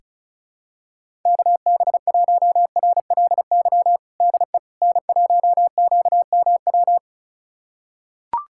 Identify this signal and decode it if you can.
Morse code, heard as a keyed tone, about 36 wpm, 690 Hz. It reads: KB1RLY DE N1QMW
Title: USA Callsign DE Callsign 35wpm